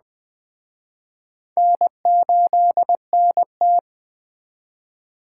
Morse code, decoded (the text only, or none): N8NT